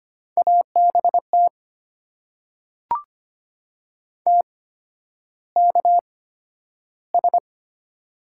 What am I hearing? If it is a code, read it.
Morse code, heard as a keyed tone, about 25 wpm, spaced as for 11 wpm, 700 Hz. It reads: ABT TKS
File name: Top 90 QSO Elements - 8x Character Spacing - 25wpm